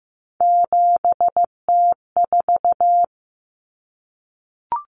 Morse code, heard as a keyed tone, about 15 wpm, 700 Hz. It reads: 7T4
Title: Common Cut Numbers - 3-Digit 15wpm